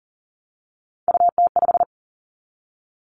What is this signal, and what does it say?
Morse code, heard as a keyed tone, about 40 wpm, 700 Hz. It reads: UT5
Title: Callsign Prefixes for Top 25 DXCC - Repeated 3x - 40wpm